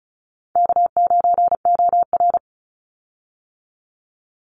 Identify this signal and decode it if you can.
Morse code, heard as a keyed tone, about 34 wpm, 700 Hz. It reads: K9OR
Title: Call Signs - 35wpm